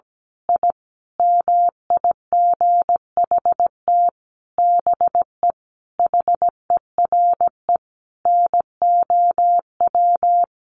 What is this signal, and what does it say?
Morse code, heard as a keyed tone, about 17 wpm, 700 Hz. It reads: I MIGHT BE HERE NOW
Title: A Sound Of Thunder 17wpm